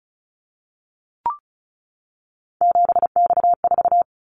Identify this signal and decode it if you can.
Morse code, heard as a keyed tone, about 36 wpm, 700 Hz. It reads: 7X4
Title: Callsign Prefixes for Top 100 DXCC - 35wpm